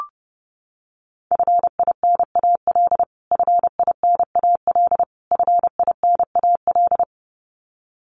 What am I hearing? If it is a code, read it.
Morse code, heard as a keyed tone, about 30 wpm, 700 Hz. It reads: FINAL FINAL FINAL
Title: Top 500 Words Repeated 30wpm